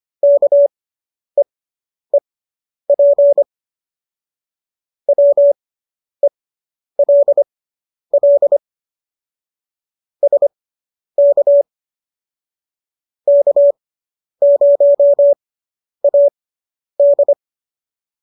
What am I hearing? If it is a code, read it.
Morse code, heard as a keyed tone, about 25 wpm, spaced as for 10 wpm, 570 Hz. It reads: KEEP WELL SK K0AD